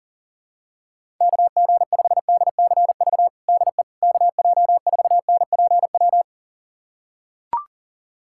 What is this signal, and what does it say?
Morse code, heard as a keyed tone, about 40 wpm, 700 Hz. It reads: KG5DCV DE KJ4NPW
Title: USA Callsign DE Callsign - 6 Characters - 40wpm